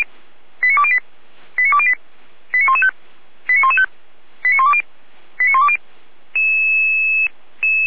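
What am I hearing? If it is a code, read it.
Five-tone selective call (ZVEI), heard as quick runs of five tones, tones of about 70 ms. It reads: 89208 89208 89115 89115 89120 89120
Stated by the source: Received on an SDR.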